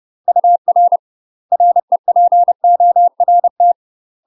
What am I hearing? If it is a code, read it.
Morse code, heard as a keyed tone, about 30 wpm, 700 Hz. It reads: UR REPORT
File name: UK MASTER QSOs TRAINING / 30WPM / UK MASTER QSOs 0022